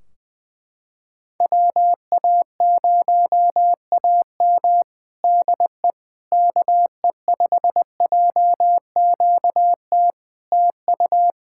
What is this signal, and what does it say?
Morse code, heard as a keyed tone, about 20 wpm, 700 Hz. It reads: WA0AM DE KE5JQT TU
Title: Practice QSO - Exchange 4 - Ending - 20wpm